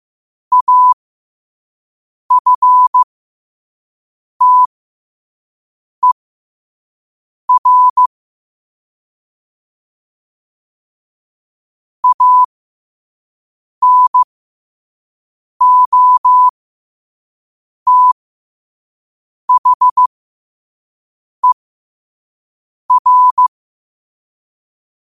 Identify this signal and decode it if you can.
Morse code, heard as a keyed tone, about 15 wpm, spaced as for 5 wpm, 1000 Hz. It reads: AFTER ANOTHER
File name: qotdfiles / 5wpm